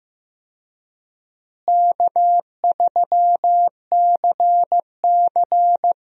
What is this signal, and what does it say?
Morse code, heard as a keyed tone, about 15 wpm, 700 Hz. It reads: K3CC
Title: USA Call Signs - 4 Characters - Repeated 2x - 15wpm